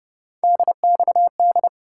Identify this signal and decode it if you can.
Morse code, heard as a keyed tone, about 30 wpm, 700 Hz. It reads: DXD
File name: Speed-Racing - Three Letter-Number Combinations - 25wpm